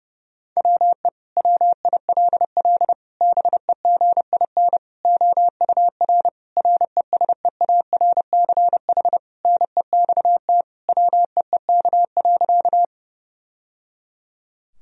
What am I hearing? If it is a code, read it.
Morse code, heard as a keyed tone, about 30 wpm, 700 Hz. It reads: WE WILL BEGIN OUR RESEARCH NEXT WEEK.
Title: Sentences from Top 1000 Words - Review Part 4 - 30wpm